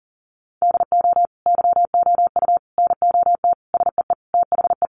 Morse code, heard as a keyed tone, about 40 wpm, 700 Hz. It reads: DO YOU NOT SEE THE
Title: Sentences from Top 800 Words 40wpm